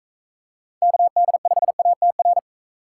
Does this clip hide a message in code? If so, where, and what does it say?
Yes: Morse code from 0.8 to 2.4 s, heard as a keyed tone, about 38 wpm, 700 Hz. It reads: KD5ATR